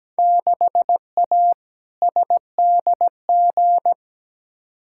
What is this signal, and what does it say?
Morse code, heard as a keyed tone, about 17 wpm, 700 Hz. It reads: 6A SDG
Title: Field Day Exchanges Encore - 17wpm